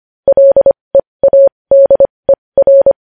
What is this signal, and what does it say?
Morse code, heard as a keyed tone, about 25 wpm, spaced as for 20 wpm, 550 Hz. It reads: LEADER
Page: news-headlines-in-morse-code-at-20-wpm